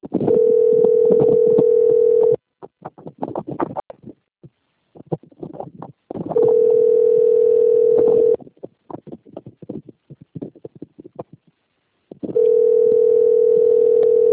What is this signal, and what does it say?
Ringback tone (North America), 440 + 480 Hz.
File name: out- - - - - .8703